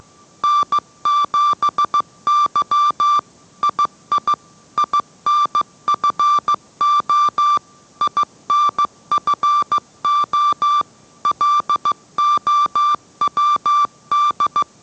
Morse code, heard as a keyed tone, about 15 wpm, 1170 Hz. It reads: N7Y II INFO INFO LOWD